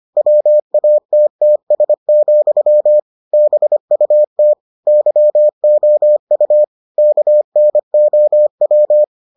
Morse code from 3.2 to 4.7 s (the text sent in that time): BUT